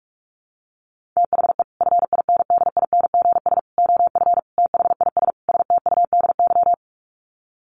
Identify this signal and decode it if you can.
Morse code, heard as a keyed tone, about 44 wpm, 700 Hz. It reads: THE FINDINGS OF THIS STUDY
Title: Sets of 5 Words Encore2 45wpm